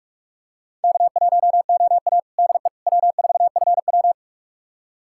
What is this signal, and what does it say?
Morse code, heard as a keyed tone, about 46 wpm, 700 Hz. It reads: K1OA DE W4FW
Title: USA Callsign DE Callsign - 4 Characters - 45wpm